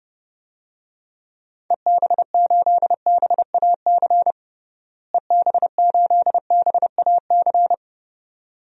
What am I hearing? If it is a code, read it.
Morse code, heard as a keyed tone, about 30 wpm, 700 Hz. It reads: EB8BAC EB8BAC
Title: DXCC-100 Call Signs - 6 Characters - Repeated 2x - 30wpm